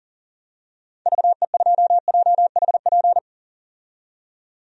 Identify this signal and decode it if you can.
Morse code, heard as a keyed tone, about 40 wpm, 700 Hz. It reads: VE2JHP
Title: DXCC-100 Call Signs - 6 Characters - 40wpm